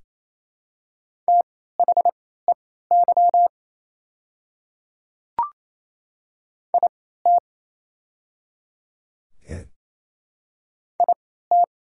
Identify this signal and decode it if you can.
Morse code, heard as a keyed tone, about 28 wpm, spaced as for 16 wpm, 690 Hz. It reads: THEY IT IT
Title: 3x Character Spacing - Top 100 Words - 28wpm